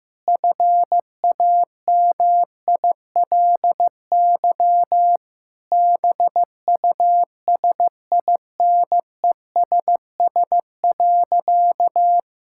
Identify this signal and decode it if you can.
Morse code, heard as a keyed tone, about 15 wpm, 700 Hz. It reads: FAMILY BUSINESS.
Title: Sentences from Top 1000 Words - Review Part 6 - 15wpm